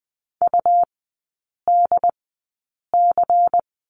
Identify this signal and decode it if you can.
Morse code, heard as a keyed tone, about 20 wpm, spaced as for 8 wpm, 700 Hz. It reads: UDC
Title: 191 Lesson 31e - Letter v - Three Characters 20wpm